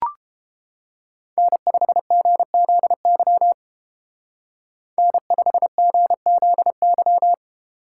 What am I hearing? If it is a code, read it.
Morse code, heard as a keyed tone, about 30 wpm, 700 Hz. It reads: N5GZY N5GZY